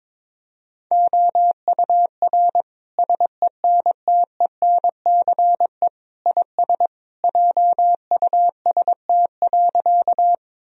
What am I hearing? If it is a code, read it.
Morse code, heard as a keyed tone, about 22 wpm, 700 Hz. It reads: OUR SENTENCE IS JUST.
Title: Sentences from Top 1000 Words - Review Part 1 - 22wpm